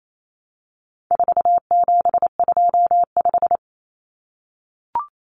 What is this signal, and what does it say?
Morse code, heard as a keyed tone, about 28 wpm, 700 Hz. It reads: 4725
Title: Numbers - 4-Digit 28wpm